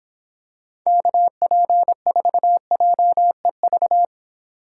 Morse code, heard as a keyed tone, about 25 wpm, 700 Hz. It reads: KP4JEV